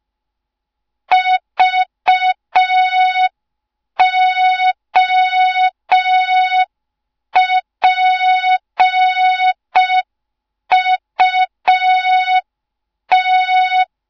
Morse code, heard as a keyed tone, about 5 wpm, 3700 Hz. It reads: VOPUT
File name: Morze-HA2MN / hang / 07 GYAKORLAT